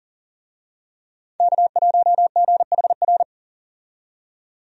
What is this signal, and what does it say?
Morse code, heard as a keyed tone, about 40 wpm, 700 Hz. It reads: K1GHR
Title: USA Call Signs - 5 Characters - 40wpm